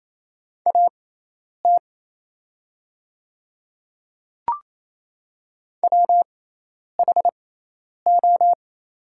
Morse code, heard as a keyed tone, about 28 wpm, spaced as for 10 wpm, 700 Hz. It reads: AT WHO
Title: 6x Character Spacing - Top 100 Words - 28wpm